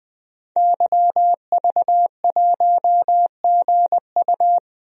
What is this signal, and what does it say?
Morse code, heard as a keyed tone, about 20 wpm, 700 Hz. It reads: YV1GU
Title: DXCC-100 Call Signs - 5 Characters - 20wpm